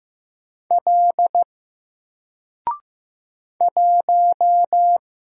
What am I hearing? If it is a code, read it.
Morse code, heard as a keyed tone, about 15 wpm, 700 Hz. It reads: L 1
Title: Single Character (Letter:Number:Punctuation:Prosign) Alt 15wpm